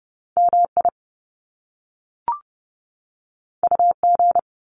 Morse code, heard as a keyed tone, about 30 wpm, 700 Hz. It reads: MI UG
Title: Top Two Letter Combinations in Words 30wpm